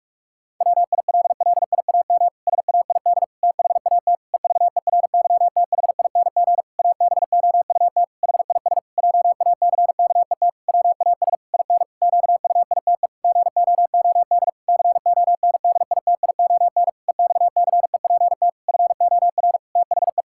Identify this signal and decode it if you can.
Morse code, heard as a keyed tone, about 46 wpm, 700 Hz. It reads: WILLIAM SAID THAT EVERYTHING ABOUT HIS JACKET WAS IN QUITE GOOD CONDITION EXCEPT FOR THE
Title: Pangram Challenge v2 - 45wpm